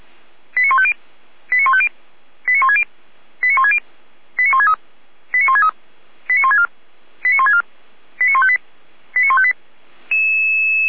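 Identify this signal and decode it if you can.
Five-tone selective call (ZVEI), heard as quick runs of five tones, tones of about 70 ms. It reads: 89150 89150 89160 89160 89163 89163 89164 89164 89158 89158
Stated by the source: ZVEI five-tone call 89121 received by SDR.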